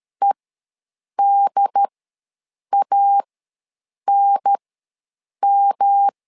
Morse code, heard as a keyed tone, about 13 wpm, spaced as for 7 wpm, 800 Hz. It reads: EDANM
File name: LEZIONI / 3 MWRO / 3 MWR ws 12wpm 1809